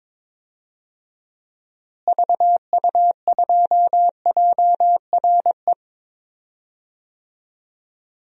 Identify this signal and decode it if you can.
Morse code, heard as a keyed tone, about 21 wpm, 700 Hz. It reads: VU2JRE
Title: DXCC-100 Call Signs - 6 Characters - Repeated 3x - 22wpm